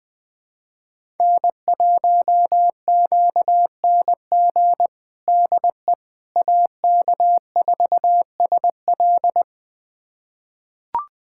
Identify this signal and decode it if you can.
Morse code, heard as a keyed tone, about 20 wpm, 700 Hz. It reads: N1QNG DE AK4SL
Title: USA Callsign DE Callsign - 5 Characters - 20wpm